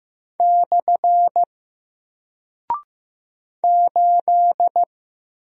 Morse code, heard as a keyed tone, about 15 wpm, 700 Hz. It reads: / 8
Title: Single Character (Letter:Number:Punctuation:Prosign) 15wpm